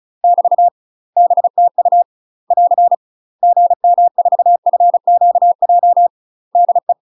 Morse code, heard as a keyed tone, about 36 wpm, 700 Hz. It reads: <BT> BTU <AR> GM4FQJ DE
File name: UK MASTER QSOs TRAINING / 35WPM / UK MASTER QSOs 0016